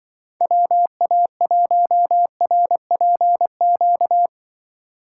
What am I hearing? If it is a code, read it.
Morse code, heard as a keyed tone, about 24 wpm, 700 Hz. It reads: WA1RPQ
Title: Speed Racing - USA 2x3 Call Signs 20wpm